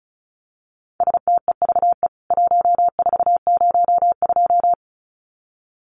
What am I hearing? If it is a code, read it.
Morse code, heard as a keyed tone, about 34 wpm, 700 Hz. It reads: STEVE 1402
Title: CWT Exchange 35wpm